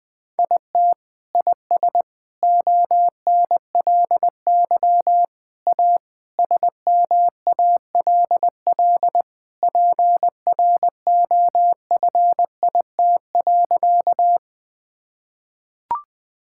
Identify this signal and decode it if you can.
Morse code, heard as a keyed tone, about 20 wpm, 700 Hz. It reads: IT IS ONLY A SMALL PROFIT.
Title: Sentences from Top 1000 Words - Review Part 5 - 20wpm